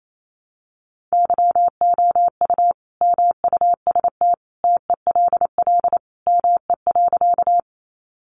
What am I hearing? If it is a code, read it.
Morse code, heard as a keyed tone, about 28 wpm, 700 Hz. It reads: YOU MUST TELL ME.
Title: Sentences from Top 200 Words 28wpm